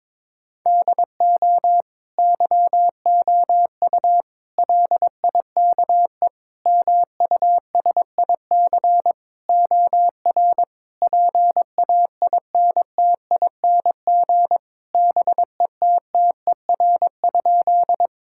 Morse code, heard as a keyed tone, about 22 wpm, 700 Hz. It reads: DO YOU LIKE MUSIC OR PAINTING BETTER?
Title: Sentences from Top 1000 Words - Review Part 6 - 22wpm